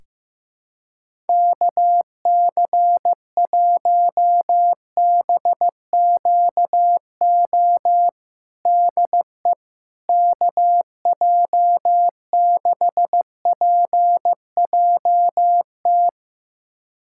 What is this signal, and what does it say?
Morse code, heard as a keyed tone, about 15 wpm, 700 Hz. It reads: KC1BQO DE KJ6PJT
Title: USA Callsign DE Callsign - 6 Characters - 15wpm